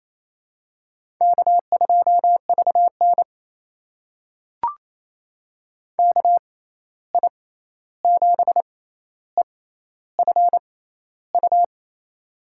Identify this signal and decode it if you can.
Morse code, heard as a keyed tone, about 28 wpm, spaced as for 13 wpm, 700 Hz. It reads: K2VN KI7EFU
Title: USA Call Signs - 6x Character Spacing - 28wpm